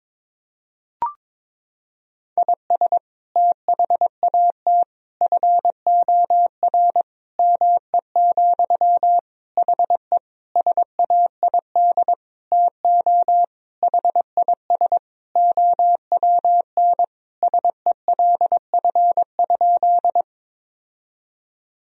Morse code, heard as a keyed tone, about 22 wpm, 700 Hz. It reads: IS THAT FOR ME, HE SAID TO HIS OWN SELF?
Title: Sentences from Top 1000 Words - Review Part 1 - 22wpm